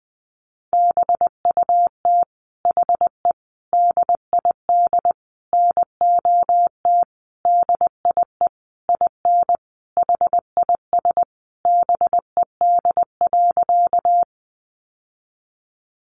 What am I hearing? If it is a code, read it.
Morse code, heard as a keyed tone, about 20 wpm, 700 Hz. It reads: BUT HE DID NOT DIE IN HIS BED.
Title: Sentences from Top 1300 Words 20wpm